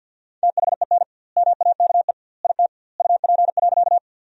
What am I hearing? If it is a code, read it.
Morse code, heard as a keyed tone, about 50 wpm, 700 Hz. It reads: THEN MAKE IT UP.
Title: 2x Word-Spacing - Sentences from Top 100 Words - 50wpm